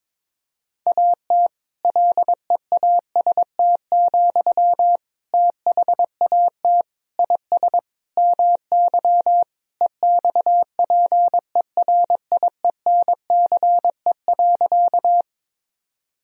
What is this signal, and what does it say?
Morse code, heard as a keyed tone, about 21 wpm, 700 Hz. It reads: AT LEAST, THAT IS MY EXPERIENCE.
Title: Sentences from Top 1000 Words - Review Part 3 - 22wpm